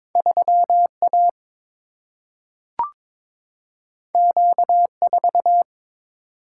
Morse code, heard as a keyed tone, about 22 wpm, 700 Hz. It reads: 3 A Q 4
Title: Single-Letter Single-Number 22wpm